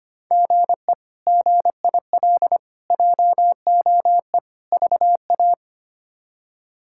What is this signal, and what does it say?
Morse code, heard as a keyed tone, about 25 wpm, 700 Hz. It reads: GE GIL JOE VA
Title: K1USN SST QSO Practice - 25wpm